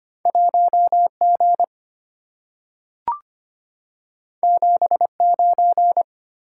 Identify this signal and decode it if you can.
Morse code, heard as a keyed tone, about 25 wpm, 700 Hz. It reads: 1G 79
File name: Two Letter-Number Combinations - 25wpm